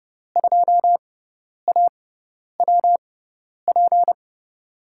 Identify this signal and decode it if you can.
Morse code, heard as a keyed tone, about 32 wpm, spaced as for 10 wpm, 700 Hz. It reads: 2AWP
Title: USA Call Signs - 6x Character Spacing - 30wpm